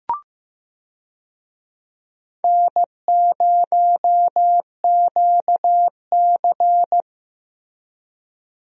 Morse code, heard as a keyed tone, about 15 wpm, 700 Hz. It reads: N0QC
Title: USA Call Signs - 4 Characters - Repeated 3x - 15wpm